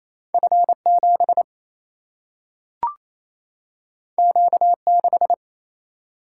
Morse code, heard as a keyed tone, about 28 wpm, 700 Hz. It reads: F 7 Q 6
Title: Single-Letter Single-Number 28wpm